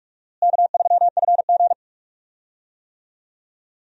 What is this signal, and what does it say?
Morse code, heard as a keyed tone, about 42 wpm, 700 Hz. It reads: K3FG
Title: USA Call Signs - 4 Characters - Repeated 2x - 45wpm